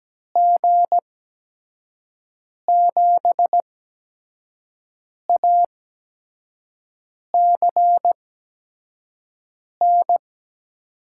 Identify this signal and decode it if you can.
Morse code, heard as a keyed tone, about 17 wpm, spaced as for 5 wpm, 700 Hz. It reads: G7ACN
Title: USA Call Signs - 6 Characters - 8x Character Spacing - 17wpm